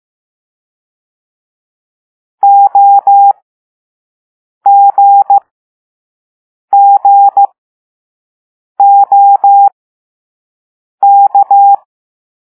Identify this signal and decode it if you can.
Morse code, heard as a keyed tone, about 15 wpm, spaced as for 5 wpm, 800 Hz. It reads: OGGOK
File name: Morse / Level4B